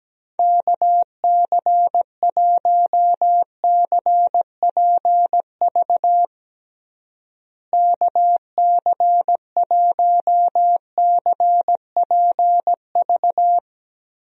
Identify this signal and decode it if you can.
Morse code, heard as a keyed tone, about 17 wpm, 700 Hz. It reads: KC1CPV KC1CPV
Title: USA Call Signs - 6 Characters - Repeated 3x - 17wpm